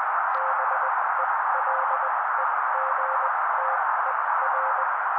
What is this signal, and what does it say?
Morse code, heard as a keyed tone, about 20 wpm, spaced as for 16 wpm, 530 Hz. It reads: BELEGTER
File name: daily / 16wpm noise